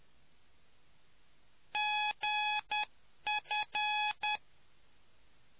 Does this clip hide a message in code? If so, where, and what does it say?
Yes: Morse code from 1.7 to 4.4 s, heard as a keyed tone, about 10 wpm, 840 Hz. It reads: GF